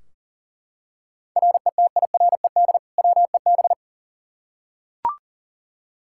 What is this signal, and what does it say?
Morse code, heard as a keyed tone, about 40 wpm, 700 Hz. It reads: RETIRED WEB
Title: Practice QSO - Exchange 3 - More Info - 40wpm